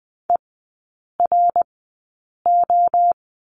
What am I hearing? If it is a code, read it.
Morse code, heard as a keyed tone, about 20 wpm, spaced as for 8 wpm, 700 Hz. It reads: ERO
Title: 053 Lesson 10e - Letter r - Three Characters 20wpm